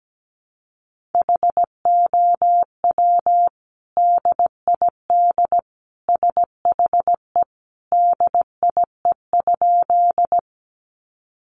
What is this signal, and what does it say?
Morse code, heard as a keyed tone, about 17 wpm, 700 Hz. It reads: HOW DID SHE DIE?
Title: Sentences from Top 1300 Words 17wpm